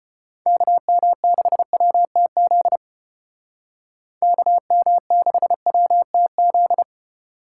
Morse code, heard as a keyed tone, about 32 wpm, 700 Hz. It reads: KM6WTZ KM6WTZ